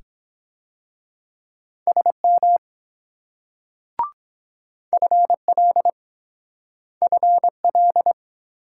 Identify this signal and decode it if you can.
Morse code, heard as a keyed tone, about 24 wpm, 700 Hz. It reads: SM FL FL